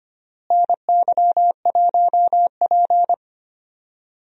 Morse code, heard as a keyed tone, about 25 wpm, 700 Hz. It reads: NY1P
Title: USA Call Signs - 4 Characters - Repeated 2x - 25wpm